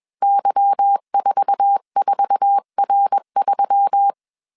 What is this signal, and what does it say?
Morse code, heard as a keyed tone, about 21 wpm, 800 Hz. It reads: Y44R3